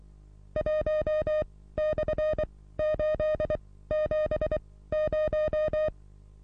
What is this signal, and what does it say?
Morse code, heard as a keyed tone, about 23 wpm, spaced as for 16 wpm, 620 Hz. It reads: 1/870
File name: repos / Morsekurs / DJ4UF Morsen27